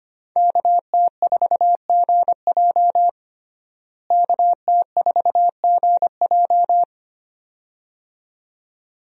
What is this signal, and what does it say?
Morse code, heard as a keyed tone, about 25 wpm, 700 Hz. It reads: KT4GJ KT4GJ
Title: USA Call Signs - 5 Characters - Repeated 2x - 25wpm